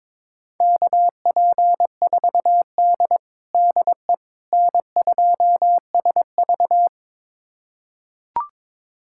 Morse code, heard as a keyed tone, about 22 wpm, 700 Hz. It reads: KP4D DE N2SV
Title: USA Callsign DE Callsign - 4 Characters - 22wpm